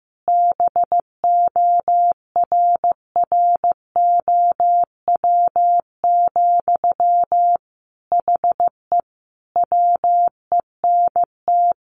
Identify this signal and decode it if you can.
Morse code, heard as a keyed tone, about 15 wpm, 700 Hz. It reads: BORROW, HE WENT
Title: Sentences from Top 1500 Words 15wpm